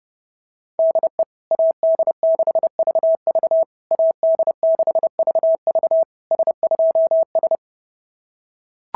Morse code, heard as a keyed tone, about 30 wpm, 650 Hz. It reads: DE AD6VV AD6VV S2S
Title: SOTA QSO Practice 30wpm